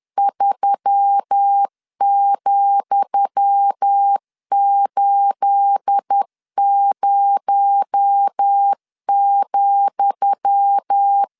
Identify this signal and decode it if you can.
Morse code, heard as a keyed tone, about 10 wpm, 800 Hz. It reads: 3,80,
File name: LEZIONI / 10 SIGNS / 10D COMA 10wpm 3013